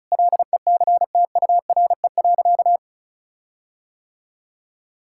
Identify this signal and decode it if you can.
Morse code, heard as a keyed tone, about 34 wpm, 700 Hz. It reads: LECTURE.
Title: Sentences from Top 1600 Words 35wpm